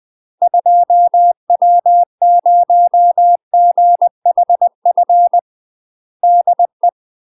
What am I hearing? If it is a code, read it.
Morse code, heard as a keyed tone, about 20 wpm, 700 Hz. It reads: 2W0GHF DE
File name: UK MASTER QSOs TRAINING/20WPM